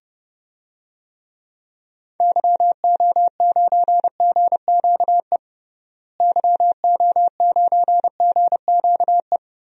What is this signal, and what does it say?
Morse code, heard as a keyed tone, about 32 wpm, 700 Hz. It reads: YO9GQE YO9GQE
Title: DXCC-100 Call Signs - 6 Characters - Repeated 2x - 30wpm